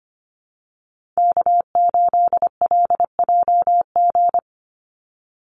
Morse code, heard as a keyed tone, about 25 wpm, 700 Hz. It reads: K8LJG
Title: NA SOTA Chaser Callsigns 25wpm